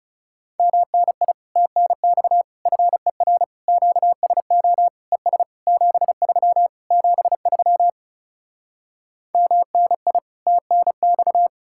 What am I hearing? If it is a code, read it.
Morse code, heard as a keyed tone, about 40 wpm, 700 Hz. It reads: MNI TNX FER QSO ES 73 73 MNI TNX